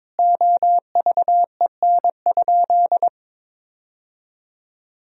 Morse code, heard as a keyed tone, about 23 wpm, 700 Hz. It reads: OVEN?
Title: Sentences from Top 1000 Words - Review Part 4 - 22wpm